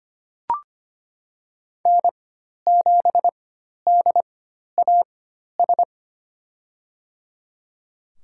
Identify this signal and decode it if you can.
Morse code, heard as a keyed tone, about 25 wpm, spaced as for 12 wpm, 700 Hz. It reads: N7DAS
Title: USA Call Signs - 4x Character Spacing - 25wpm